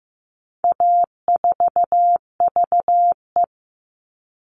Morse code, heard as a keyed tone, about 15 wpm, 700 Hz. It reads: A4VE
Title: All Cut Numbers - 4-Digit 15wpm